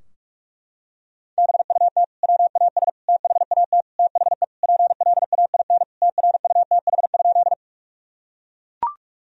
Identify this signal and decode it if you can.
Morse code, heard as a keyed tone, about 41 wpm, 700 Hz. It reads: BUT WAS THAT THE PLAIN TRUTH?
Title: Sentences from Top 1000 Words - Review Part 2 - 45wpm